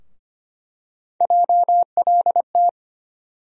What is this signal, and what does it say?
Morse code, heard as a keyed tone, about 25 wpm, 700 Hz. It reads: JLT
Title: USA Call Sign Postfixes - 3 Letters 25wpm